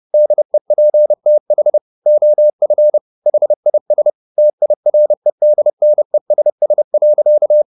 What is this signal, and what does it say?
Morse code, heard as a keyed tone, about 30 wpm, 600 Hz. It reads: DEPTH OF HIS TIREDNESS.